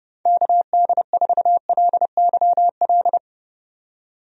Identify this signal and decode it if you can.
Morse code, heard as a keyed tone, about 30 wpm, 700 Hz. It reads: KD4LYL
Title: USA Call Signs - 6 Characters - 30wpm